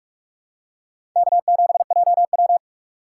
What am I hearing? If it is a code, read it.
Morse code, heard as a keyed tone, about 47 wpm, 700 Hz. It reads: K7JW
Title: USA Call Signs - 2x Character Spacing - 45wpm